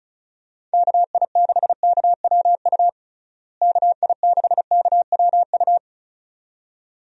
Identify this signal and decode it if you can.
Morse code, heard as a keyed tone, about 36 wpm, 700 Hz. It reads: KI6KWU KI6KWU
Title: USA Call Signs - 6 Characters - Repeated 2x - 35wpm